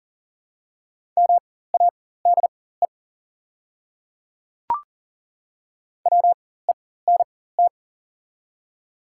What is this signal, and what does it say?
Morse code, heard as a keyed tone, about 40 wpm, spaced as for 19 wpm, 690 Hz. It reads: MADE WENT
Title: 4x Character Spacing - Top 300 Words - 40wpm